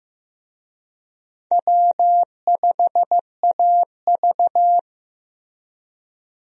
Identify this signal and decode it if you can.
Morse code, heard as a keyed tone, about 15 wpm, 700 Hz. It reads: W5AV
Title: USA Call Signs - 4 Characters - Repeated 3x - 15wpm